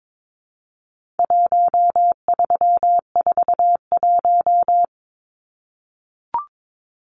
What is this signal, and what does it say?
Morse code, heard as a keyed tone, about 22 wpm, 700 Hz. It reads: 1341
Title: Numbers - 4-Digit 22wpm